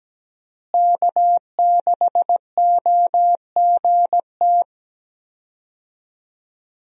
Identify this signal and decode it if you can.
Morse code, heard as a keyed tone, about 17 wpm, 700 Hz. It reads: K6OGT
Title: USA Call Signs - 5 Characters - Repeated 3x - 17wpm